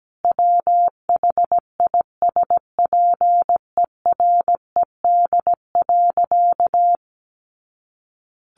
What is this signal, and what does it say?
Morse code, heard as a keyed tone, about 17 wpm, 700 Hz. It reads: WHISPERED.
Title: A Sound Of Thunder 17wpm